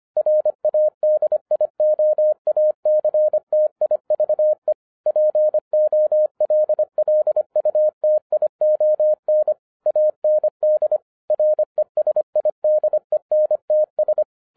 Morse code, heard as a keyed tone, about 25 wpm, 610 Hz. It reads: RADIOACTIVE POLLUTION AND RESIDENTS
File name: 2011-07-mp3 / n39 THE 3.11 2 25wpm